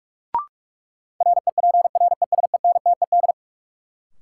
Morse code, heard as a keyed tone, about 46 wpm, 700 Hz. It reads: REPRESENTED
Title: Sets of 5 Words - New Word Drill - 45wpm